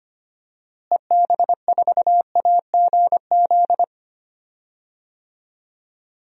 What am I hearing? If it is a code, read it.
Morse code, heard as a keyed tone, about 25 wpm, 700 Hz. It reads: EB4AGZ
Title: DXCC-100 Call Signs - 6 Characters - Repeated 3x - 25wpm